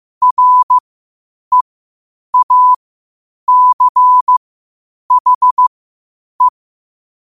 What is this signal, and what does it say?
Morse code, heard as a keyed tone, about 15 wpm, spaced as for 8 wpm, 1000 Hz. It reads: REACHE
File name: qotdfiles / 8wpm